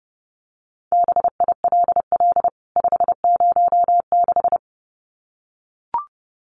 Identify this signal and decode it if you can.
Morse code, heard as a keyed tone, about 30 wpm, 700 Hz. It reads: BILL 506
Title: CWT Exchange 30wpm